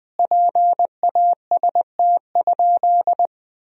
Morse code, heard as a keyed tone, about 20 wpm, 700 Hz. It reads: PAST?
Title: Sentences from Top 1000 Words - Review Part 4 - 20wpm